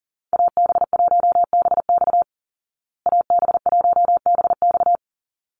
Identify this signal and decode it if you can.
Morse code, heard as a keyed tone, about 40 wpm, 700 Hz. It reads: AB1BX AB1BX
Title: Call Signs Repeated 3x - 40wpm